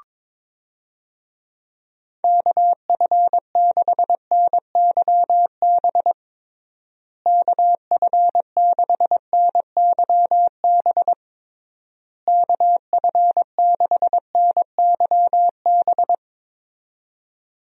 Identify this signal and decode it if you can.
Morse code, heard as a keyed tone, about 22 wpm, 700 Hz. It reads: KF6NYB KF6NYB KF6NYB
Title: USA Call Signs - 6 Characters - Repeated 3x - 22wpm